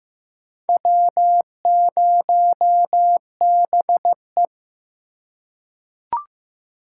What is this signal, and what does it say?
Morse code, heard as a keyed tone, about 15 wpm, 700 Hz. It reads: W0BE
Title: USA Call Signs - 3x Character Spacing - 15wpm